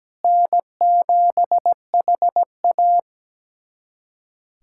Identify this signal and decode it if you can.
Morse code, heard as a keyed tone, about 17 wpm, 700 Hz. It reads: N7HA
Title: USA Callsign DE Callsign - 4 Characters - 17wpm